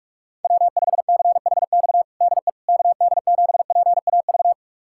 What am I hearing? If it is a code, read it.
Morse code, heard as a keyed tone, about 46 wpm, 700 Hz. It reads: W5CHX DE KD7PAV
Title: USA Callsign DE Callsign 45wpm